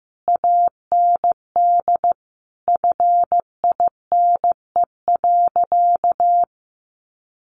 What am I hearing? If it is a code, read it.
Morse code, heard as a keyed tone, about 15 wpm, 700 Hz. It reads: AND FINE.
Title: Sentences from Top 1200 Words 15wpm